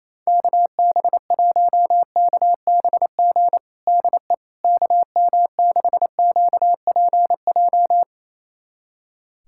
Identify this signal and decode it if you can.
Morse code, heard as a keyed tone, about 28 wpm, 700 Hz. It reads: KB1KBG DE KM6QPJ
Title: USA Callsign DE Callsign 28wpm